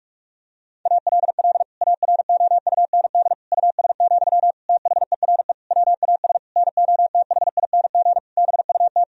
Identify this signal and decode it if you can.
Morse code, heard as a keyed tone, about 46 wpm, 700 Hz. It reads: ALL AROUND US, THERE WAS NOTHING BUT
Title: Sentences from Top 1000 Words - Review Part 3 - 45wpm